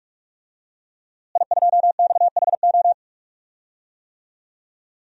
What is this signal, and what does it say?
Morse code, heard as a keyed tone, about 43 wpm, 700 Hz. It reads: I2XHO
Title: DXCC-100 Call Signs - 5 Characters - Repeated 3x - 45wpm